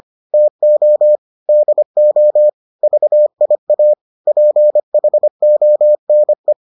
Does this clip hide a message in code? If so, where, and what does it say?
Yes: Morse code from 0.3 to 6.5 s, heard as a keyed tone, about 25 wpm, 600 Hz. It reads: TO DO VIA PHONE